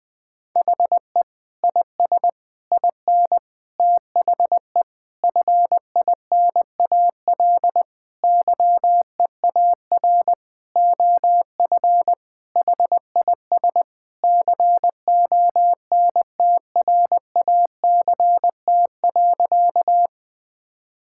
Morse code, heard as a keyed tone, about 20 wpm, 700 Hz. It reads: HE IS IN THE FINAL YEAR OF HIS CONTRACT.
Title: Sentences with 8-Word Phrases - 20wpm